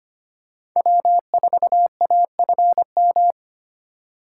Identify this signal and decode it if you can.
Morse code, heard as a keyed tone, about 25 wpm, 700 Hz. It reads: W4AFM
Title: USA Call Signs - 5 Characters - 6x Character Spacing - 25wpm